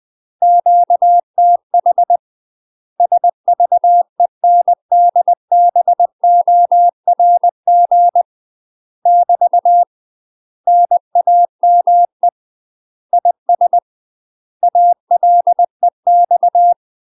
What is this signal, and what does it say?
Morse code, heard as a keyed tone, about 20 wpm, 700 Hz. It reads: QTH SVENDBORG <BT> NAME IS ALEX
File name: UK MASTER QSOs TRAINING/20WPM